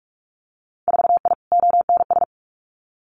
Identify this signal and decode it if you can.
Morse code, heard as a keyed tone, about 46 wpm, 700 Hz. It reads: 4I ONS
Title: Winter Field Day Exchanges Encore 45wpm